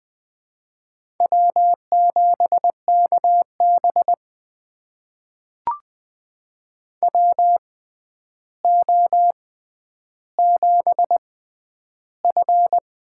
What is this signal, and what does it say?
Morse code, heard as a keyed tone, about 20 wpm, spaced as for 10 wpm, 700 Hz. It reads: W7KB WO7F
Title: USA Call Signs - 4 Characters - 6x Character Spacing - 20wpm